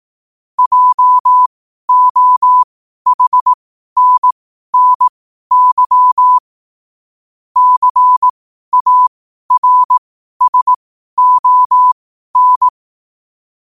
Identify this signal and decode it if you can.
Morse code, heard as a keyed tone, about 18 wpm, spaced as for 12 wpm, 1000 Hz. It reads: JOHNNY CARSON
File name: qotdfiles / 12wpm